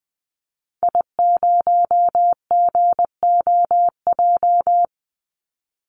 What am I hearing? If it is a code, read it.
Morse code, heard as a keyed tone, about 20 wpm, 700 Hz. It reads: I0GOJ
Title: Call Signs - 20wpm